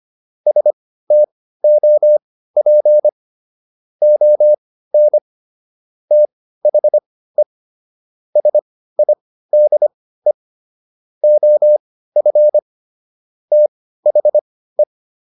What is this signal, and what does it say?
Morse code, heard as a keyed tone, about 25 wpm, spaced as for 15 wpm, 600 Hz. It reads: STOP ON THE SIDE OF THE